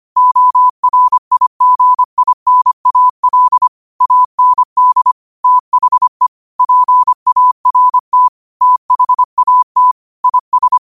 Morse code, heard as a keyed tone, about 25 wpm, 1000 Hz. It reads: ORIGINAL AND THE PART THAT IS
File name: 25wpm